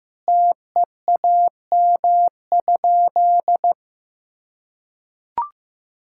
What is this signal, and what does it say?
Morse code, heard as a keyed tone, about 15 wpm, 700 Hz. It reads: TEAM?
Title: Sentences from Top 1600 Words 15wpm